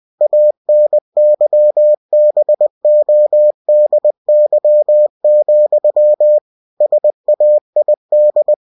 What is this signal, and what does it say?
Morse code, heard as a keyed tone, about 20 wpm, 600 Hz. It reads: ANYBODY, SAID
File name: cw-books / winnie-the-pooh / winnie-20wpm-page-0080